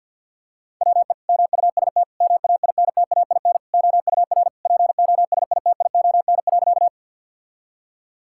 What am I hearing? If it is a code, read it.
Morse code, heard as a keyed tone, about 50 wpm, 700 Hz. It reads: WE MUST MAINTAIN OUR POSITION.
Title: Sentences from Top 1000 Words - Review Part 5 - 50wpm